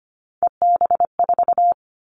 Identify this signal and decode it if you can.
Morse code, heard as a keyed tone, about 25 wpm, 700 Hz. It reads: EB4
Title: Callsign Prefixes for Top 25 DXCC - 25wpm